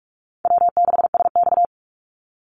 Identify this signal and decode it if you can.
Morse code, heard as a keyed tone, about 46 wpm, 700 Hz. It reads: W6SX
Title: Easy Call Signs - 45wpm